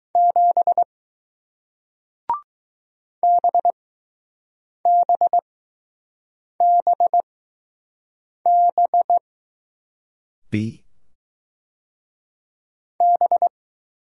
Morse code, heard as a keyed tone, about 20 wpm, spaced as for 6 wpm, 700 Hz. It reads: 7 BBBB B